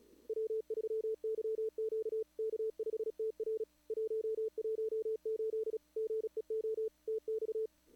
Morse code, heard as a keyed tone, about 32 wpm, 410 Hz. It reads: W3YQ K5TR 118 GEO TX
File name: CWSprint / Feb2022 / k5tr-feb-na-sprint-cw-k5ot -0000